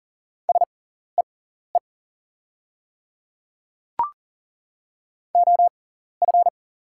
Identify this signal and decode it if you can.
Morse code, heard as a keyed tone, about 40 wpm, spaced as for 14 wpm, 690 Hz. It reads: SEE OF
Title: 6x Character Spacing - Top 100 Words - 40wpm